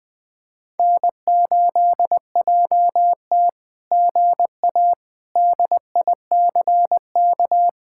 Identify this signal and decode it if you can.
Morse code, heard as a keyed tone, about 19 wpm, 700 Hz. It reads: N8JT GA DICK
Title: SOTA QSO Practice 20wpm